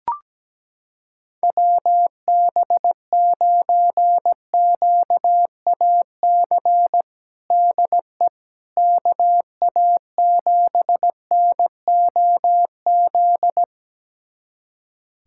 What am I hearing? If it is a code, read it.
Morse code, heard as a keyed tone, about 17 wpm, 700 Hz. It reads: WB9QAC DE KA7NOZ
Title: USA Callsign DE Callsign - 6 Characters - 17wpm